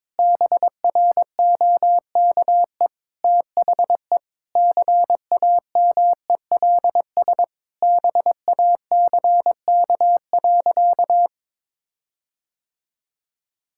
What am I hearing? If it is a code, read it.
Morse code, heard as a keyed tone, about 21 wpm, 700 Hz. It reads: BROKE THE CAMELS BACK.
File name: practice / 22wpm / Sentences with 8-Word Phrases - 22wpm